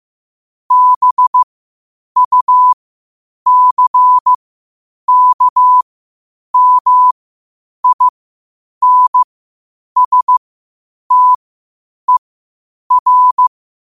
Morse code, heard as a keyed tone, about 15 wpm, spaced as for 8 wpm, 1000 Hz. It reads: BUCKMINSTER
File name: qotdfiles / 8wpm